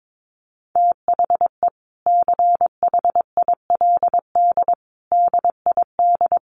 Morse code, heard as a keyed tone, about 22 wpm, 700 Hz. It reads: THE CHILD DID
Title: Sentences from Top 800 Words 22wpm